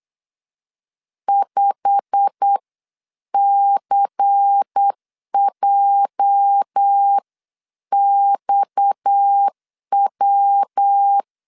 Morse code, heard as a keyed tone, about 8 wpm, spaced as for 7 wpm, 800 Hz. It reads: 5CJXW
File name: CW / DET / 1 9 1500C S C 8wpm 5653